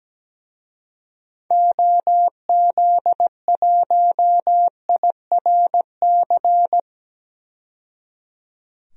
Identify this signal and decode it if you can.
Morse code, heard as a keyed tone, about 17 wpm, 700 Hz. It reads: OZ1IRC
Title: DXCC-100 Call Signs - 6 Characters - 17wpm